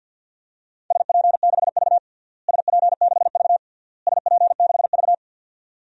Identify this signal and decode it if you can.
Morse code, heard as a keyed tone, about 47 wpm, 700 Hz. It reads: SP6V SP6V SP6V
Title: DXCC-100 Call Signs - 4 Characters - Repeated 3x - 50wpm